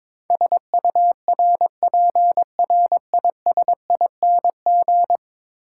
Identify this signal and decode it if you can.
Morse code, heard as a keyed tone, about 22 wpm, 700 Hz. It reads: SURPRISING
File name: practice / 22wpm / 10 Letter Words - First 3 Letters Given - 22wpm